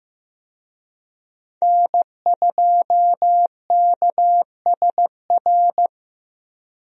Morse code, heard as a keyed tone, about 15 wpm, 700 Hz. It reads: N2KSR
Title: USA Call Signs - 5 Characters - Repeated 3x - 15wpm